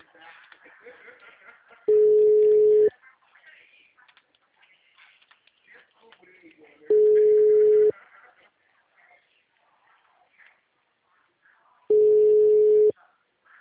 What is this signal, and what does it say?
Ringback tone (Europe, 425 Hz), with speech.